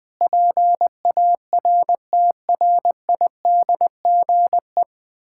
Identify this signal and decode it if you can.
Morse code, heard as a keyed tone, about 20 wpm, 700 Hz. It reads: PARTRIDGE
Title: Sentences with 9-Word Phrases - 20wpm